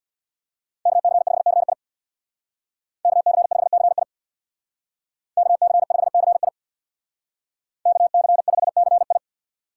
Morse code, heard as a keyed tone, about 64 wpm, 700 Hz. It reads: KC5CI KC5CI KC5CI KC5CI